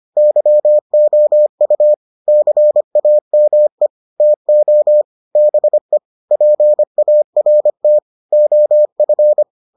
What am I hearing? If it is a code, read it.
Morse code, heard as a keyed tone, about 25 wpm, 600 Hz. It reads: YOU CAME TO BE PART OF